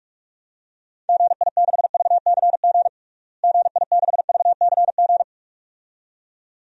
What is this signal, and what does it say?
Morse code, heard as a keyed tone, about 46 wpm, 700 Hz. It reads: GI6VCG GI6VCG
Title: DXCC-100 Call Signs - 6 Characters - Repeated 2x - 45wpm